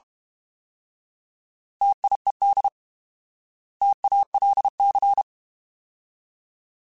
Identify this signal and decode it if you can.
Morse code, heard as a keyed tone, about 32 wpm, 800 Hz. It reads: TIED TALC
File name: Alphabetical / Alphabetical mp3s / Ts @30